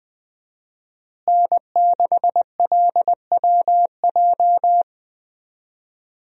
Morse code, heard as a keyed tone, about 20 wpm, 700 Hz. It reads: N6LWJ
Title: USA Call Signs - 5 Characters - Repeated 3x - 20wpm